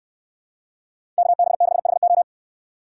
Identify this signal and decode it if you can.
Morse code, heard as a keyed tone, about 69 wpm, 700 Hz. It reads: KB6HO